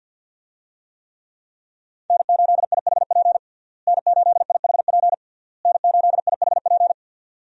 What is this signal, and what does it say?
Morse code, heard as a keyed tone, about 50 wpm, 700 Hz. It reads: N8IHP N8IHP N8IHP
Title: USA Call Signs - 5 Characters - Repeated 3x - 50wpm